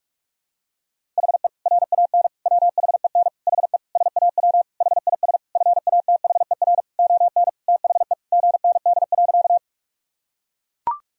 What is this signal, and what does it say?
Morse code, heard as a keyed tone, about 42 wpm, 700 Hz. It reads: HE RAN WHEN HE SAW HIS FATHER ON THE GND.
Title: Sentences from Top 500 Words and Abbreviations 45wpm